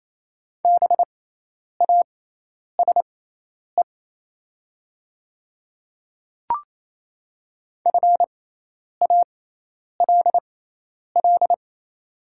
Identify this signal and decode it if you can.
Morse code, heard as a keyed tone, about 28 wpm, spaced as for 10 wpm, 700 Hz. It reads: BASE FALL
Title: 6x Character Spacing - Top 500 Words - 28wpm